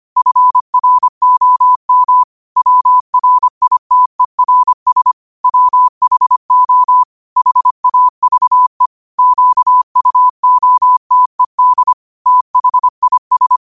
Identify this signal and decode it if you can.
Morse code, heard as a keyed tone, about 25 wpm, 1000 Hz. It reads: FROM WRITERS WHO HAVE QUOTED THIS
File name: qotdfiles / 25wpm